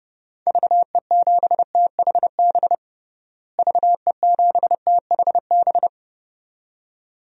Morse code, heard as a keyed tone, about 29 wpm, 700 Hz. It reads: VE7THB VE7THB
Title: DXCC-100 Call Signs - 6 Characters - Repeated 2x - 30wpm